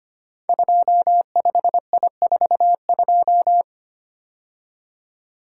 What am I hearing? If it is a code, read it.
Morse code, heard as a keyed tone, about 25 wpm, 700 Hz. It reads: 25I42
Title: Code Groups - 5 Letter:Numbers - 25wpm